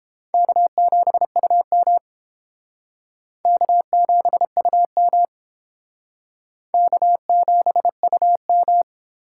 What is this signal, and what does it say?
Morse code, heard as a keyed tone, about 31 wpm, 700 Hz. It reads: K7UM K7UM K7UM